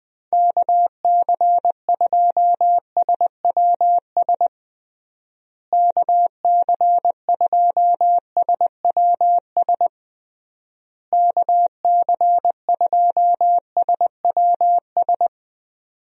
Morse code, heard as a keyed tone, about 20 wpm, 700 Hz. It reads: KC2SWS KC2SWS KC2SWS
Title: USA Call Signs - 6 Characters - Repeated 3x - 20wpm